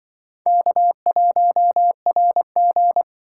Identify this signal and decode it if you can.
Morse code, heard as a keyed tone, about 25 wpm, 700 Hz. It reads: K1RG